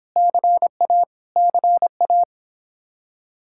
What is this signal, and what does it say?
Morse code, heard as a keyed tone, about 26 wpm, 700 Hz. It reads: CA CA